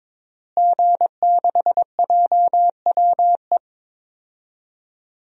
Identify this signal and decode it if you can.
Morse code, heard as a keyed tone, about 21 wpm, 700 Hz. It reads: G6JWE
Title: DXCC-100 Call Signs - 5 Characters - Repeated 3x - 22wpm